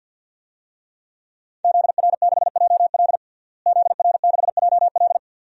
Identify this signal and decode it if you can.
Morse code, heard as a keyed tone, about 53 wpm, 700 Hz. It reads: ZR6JL ZR6JL
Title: DXCC-100 Call Signs - 5 Characters - Repeated 2x - 50wpm